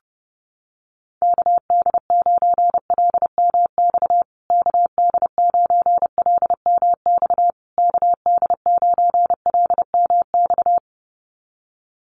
Morse code, heard as a keyed tone, about 30 wpm, 700 Hz. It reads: KD9LMX KD9LMX KD9LMX
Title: USA Call Signs Repeated 3x 30wpm